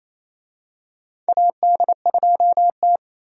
Morse code, heard as a keyed tone, about 28 wpm, 700 Hz. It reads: AD2T
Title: USA Call Signs - 4 Characters - Repeated 3x - 28wpm